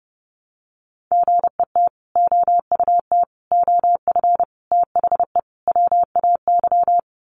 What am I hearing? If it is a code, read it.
Morse code, heard as a keyed tone, about 30 wpm, 700 Hz. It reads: GET OUT OF THE WAY
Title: Sets of 5 Words Encore 30wpm